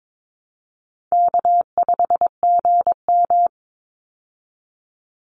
Morse code, heard as a keyed tone, about 22 wpm, 700 Hz. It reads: K5GM
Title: Easy Call Signs Repeated 3x - 22wpm